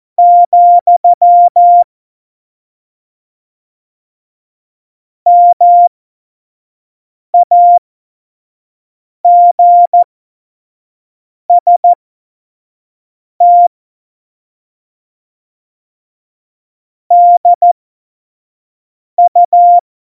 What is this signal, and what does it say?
Morse code, heard as a keyed tone, about 14 wpm, spaced as for 5 wpm, 700 Hz. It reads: , MAGST DU
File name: Morsekurs / Witze / Witz 18. Dezember 2018, 25 F70